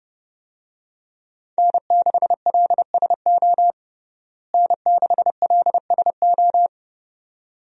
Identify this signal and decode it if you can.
Morse code, heard as a keyed tone, about 32 wpm, 700 Hz. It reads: N6LSO N6LSO
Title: USA Call Signs - 5 Characters - Repeated 2x - 30wpm